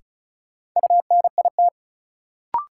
Morse code, heard as a keyed tone, about 36 wpm, 700 Hz. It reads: UNIT
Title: Sentences from Top 1000 Words - Review Part 2 - New Word Drill - 35wpm